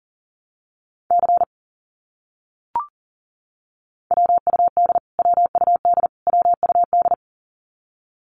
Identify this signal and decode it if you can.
Morse code, heard as a keyed tone, about 40 wpm, 700 Hz. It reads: C WUD WUD WUD
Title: Abbreviations Repeated 3x - 40wpm